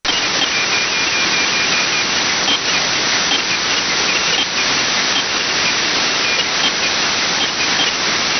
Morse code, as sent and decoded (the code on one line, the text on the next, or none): none